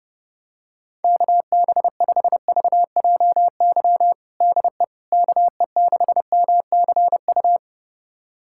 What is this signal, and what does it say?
Morse code, heard as a keyed tone, about 32 wpm, 700 Hz. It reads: KB5VJY DE KE6MCU
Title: USA Callsign DE Callsign 30wpm